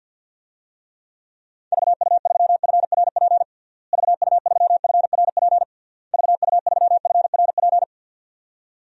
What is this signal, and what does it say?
Morse code, heard as a keyed tone, about 50 wpm, 700 Hz. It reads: VU3FRP VU3FRP VU3FRP
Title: DXCC-100 Call Signs - 6 Characters - Repeated 3x - 50wpm